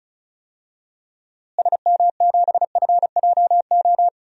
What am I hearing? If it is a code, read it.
Morse code, heard as a keyed tone, about 36 wpm, 700 Hz. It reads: SM7FJO
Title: DXCC-100 Call Signs - 6 Characters - Repeated 2x - 35wpm